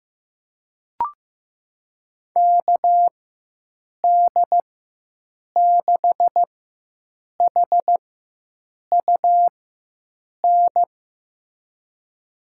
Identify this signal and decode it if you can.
Morse code, heard as a keyed tone, about 16 wpm, spaced as for 7 wpm, 700 Hz. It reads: KD6HUN
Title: USA Call Signs - 6 Characters - 4x Character Spacing - 15wpm